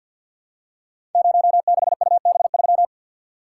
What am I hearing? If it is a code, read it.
Morse code, heard as a keyed tone, about 52 wpm, 700 Hz. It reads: 06UB3
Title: Code Groups - 5 Letter:Numbers - 50wpm